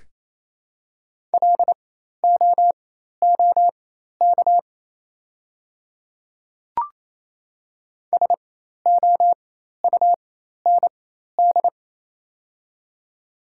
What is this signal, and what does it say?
Morse code, heard as a keyed tone, about 27 wpm, spaced as for 13 wpm, 700 Hz. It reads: LOOK SOUND
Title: 4x Character Spacing - Top 100 Words - 28wpm